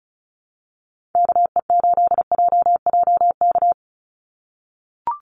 Morse code, heard as a keyed tone, about 34 wpm, 700 Hz. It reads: KE8JJK
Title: USA Call Signs 35wpm